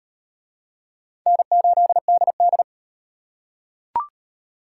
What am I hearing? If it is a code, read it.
Morse code, heard as a keyed tone, about 33 wpm, 700 Hz. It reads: N8DD